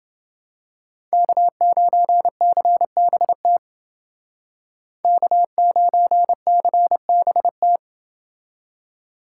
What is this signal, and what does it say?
Morse code, heard as a keyed tone, about 29 wpm, 700 Hz. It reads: K9CBT K9CBT